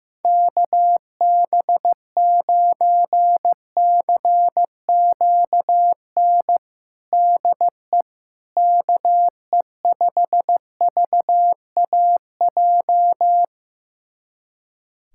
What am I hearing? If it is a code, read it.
Morse code, heard as a keyed tone, about 15 wpm, 700 Hz. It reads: KB9CQN DE KE5VAJ
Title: USA Callsign DE Callsign - 6 Characters - 15wpm